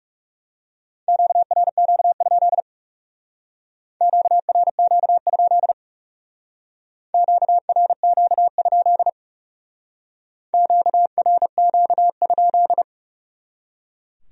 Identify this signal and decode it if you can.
Morse code, heard as a keyed tone, about 38 wpm, 700 Hz. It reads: QRQ? QRQ? QRQ? QRQ?